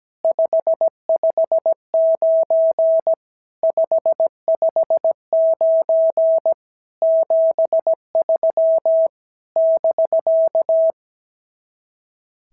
Morse code, heard as a keyed tone, about 17 wpm, 650 Hz. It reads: 559 559 73 <BK>
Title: SOTA QSO Practice 17wpm